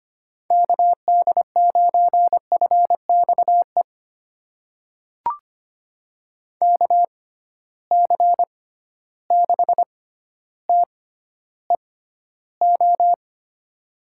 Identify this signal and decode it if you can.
Morse code, heard as a keyed tone, about 25 wpm, spaced as for 13 wpm, 700 Hz. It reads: KD9FXE KC6TEO
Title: USA Call Signs - 6 Characters - 6x Character Spacing - 25wpm